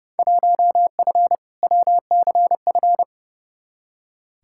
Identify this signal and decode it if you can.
Morse code, heard as a keyed tone, about 30 wpm, 700 Hz. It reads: 1F WCF
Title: Field Day Exchanges - 30wpm